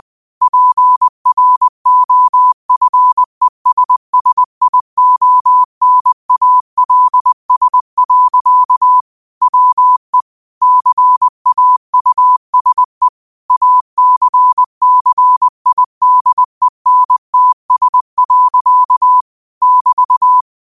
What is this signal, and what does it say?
Morse code, heard as a keyed tone, about 20 wpm, 1000 Hz. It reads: PROFESSIONALS. WE CAUSE ACCIDENTS. <BT>